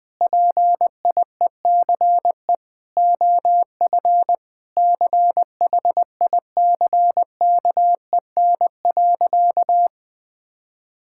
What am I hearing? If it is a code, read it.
Morse code, heard as a keyed tone, about 21 wpm, 700 Hz. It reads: PIECE OF CHICKEN.
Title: Sentences from Top 1000 Words - Review Part 5 - 20wpm